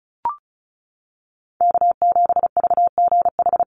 Morse code, heard as a keyed tone, about 34 wpm, 700 Hz. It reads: K7VGH
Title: USA Call Signs 35wpm